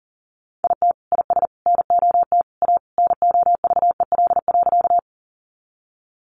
Morse code, heard as a keyed tone, about 40 wpm, 700 Hz. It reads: IT IS NOT A NOVEL.
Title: Sentences from Top 1300 Words 40wpm